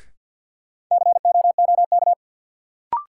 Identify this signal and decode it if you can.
Morse code, heard as a keyed tone, about 53 wpm, 700 Hz. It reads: COOK
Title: Sentences from Top 700 Words - New Word Drill - 50wpm